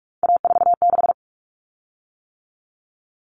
Morse code, heard as a keyed tone, about 44 wpm, 700 Hz. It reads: A46
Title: Common Cut Numbers - 3-Digit 45wpm